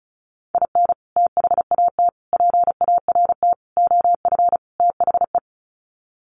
Morse code, heard as a keyed tone, about 34 wpm, 700 Hz. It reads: IN THAT PART OF THE
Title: Sets of 5 Words Encore2 35wpm